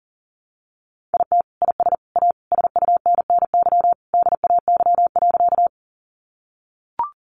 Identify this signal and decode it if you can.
Morse code, heard as a keyed tone, about 40 wpm, 700 Hz. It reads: IT IS A SUNNY DAY.
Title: Additional Easy Phrases 40wpm